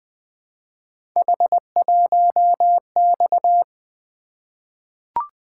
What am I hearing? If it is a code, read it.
Morse code, heard as a keyed tone, about 20 wpm, 700 Hz. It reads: H1X
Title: Three Letter-Number Combinations - 20wpm